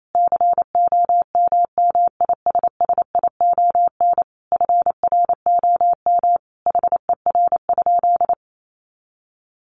Morse code, heard as a keyed tone, about 28 wpm, 700 Hz. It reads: COMMISSION FROM HER?
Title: Sentences from Top 1400 Words 28wpm